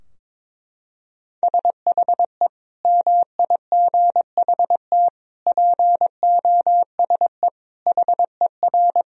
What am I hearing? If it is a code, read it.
Morse code, heard as a keyed tone, about 22 wpm, 700 Hz. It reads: SHE MIGHT POSE HER
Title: Sentences from Top 1000 Words - Review Part 2 - 22wpm